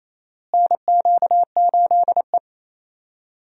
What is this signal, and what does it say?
Morse code, heard as a keyed tone, about 28 wpm, 700 Hz. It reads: NQ8E
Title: USA Call Signs - 4 Characters - Repeated 2x - 28wpm